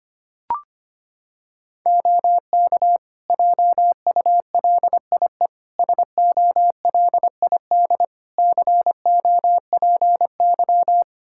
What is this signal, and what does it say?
Morse code, heard as a keyed tone, about 25 wpm, 700 Hz. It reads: OK JULIE SOLID COPY
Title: Practice QSO - Exchange 3 - More Info - 25wpm